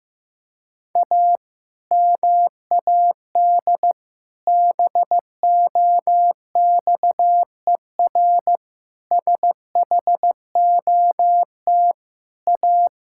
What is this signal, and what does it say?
Morse code, heard as a keyed tone, about 15 wpm, 700 Hz. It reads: A MAD BOXER SHOT A
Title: Pangram Challenge v2 - 15wpm